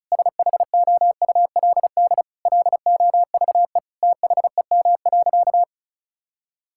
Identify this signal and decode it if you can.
Morse code, heard as a keyed tone, about 36 wpm, 700 Hz. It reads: SHOULD LOVE THEM.
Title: Sentences from Top 1600 Words 35wpm